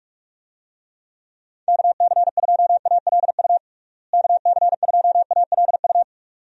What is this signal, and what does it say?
Morse code, heard as a keyed tone, about 42 wpm, 700 Hz. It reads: KC2ALU KC2ALU
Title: USA Call Signs - 6 Characters - Repeated 2x - 45wpm